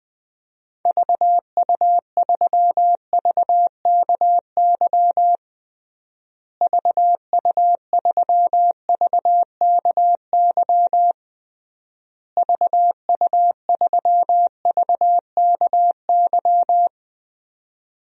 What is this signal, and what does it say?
Morse code, heard as a keyed tone, about 20 wpm, 700 Hz. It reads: VU3VKY VU3VKY VU3VKY
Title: DXCC-100 Call Signs - 6 Characters - Repeated 3x - 20wpm